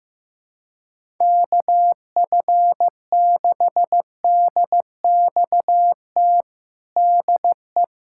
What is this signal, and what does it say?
Morse code, heard as a keyed tone, about 14 wpm, 700 Hz. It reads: KF6DXT DE
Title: USA Callsign DE Callsign - 6 Characters - 15wpm